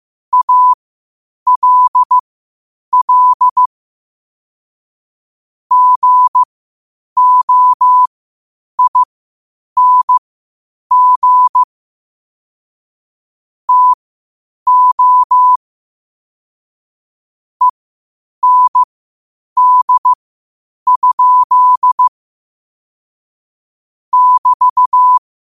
Morse code, heard as a keyed tone, about 15 wpm, spaced as for 8 wpm, 1000 Hz. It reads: ALL GOING TO END? <BT>
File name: qotdfiles / 8wpm